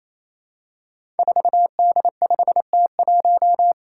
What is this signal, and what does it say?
Morse code, heard as a keyed tone, about 27 wpm, 700 Hz. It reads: 4D5T1
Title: Five Challenging Letter-Number Combinations 28wpm